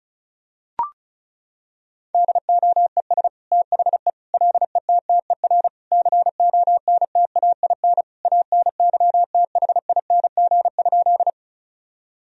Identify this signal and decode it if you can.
Morse code, heard as a keyed tone, about 34 wpm, 700 Hz. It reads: DOES THE LETTER CONTAIN ANYTHING?
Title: Sentences from Top 1000 Words - Review Part 2 - 35wpm